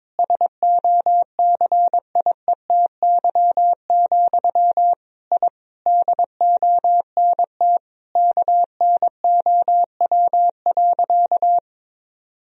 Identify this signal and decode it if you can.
Morse code, heard as a keyed tone, about 22 wpm, 700 Hz. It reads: SOCIETY, I DONT KNOW.
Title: Sentences from Top 1000 Words - Review Part 4 - 22wpm